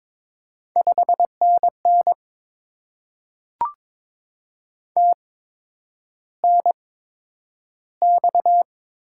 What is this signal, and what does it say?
Morse code, heard as a keyed tone, about 22 wpm, spaced as for 9 wpm, 700 Hz. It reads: 5NN TNX
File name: Top 20 QSO Elements - 8x Character Spacing - 22wpm